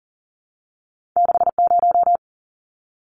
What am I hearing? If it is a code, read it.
Morse code, heard as a keyed tone, about 40 wpm, 700 Hz. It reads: 60
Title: Numbers - 2-Digit 40wpm